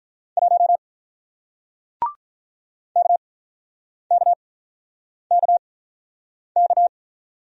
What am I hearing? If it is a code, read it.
Morse code, heard as a keyed tone, about 48 wpm, spaced as for 9 wpm, 700 Hz. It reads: 1 KKKK